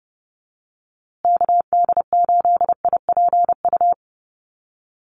Morse code, heard as a keyed tone, about 30 wpm, 700 Hz. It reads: KD8IPU
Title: USA Call Signs Repeated 2x 30wpm